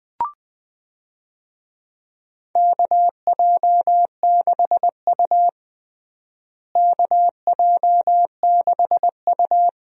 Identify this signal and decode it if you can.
Morse code, heard as a keyed tone, about 20 wpm, 700 Hz. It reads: KJ6U KJ6U
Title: USA Call Signs - 4 Characters - Repeated 2x - 20wpm